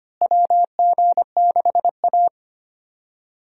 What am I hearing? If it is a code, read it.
Morse code, heard as a keyed tone, about 25 wpm, 700 Hz. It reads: WG6A
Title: USA Call Signs - 4 Characters - Repeated 2x - 25wpm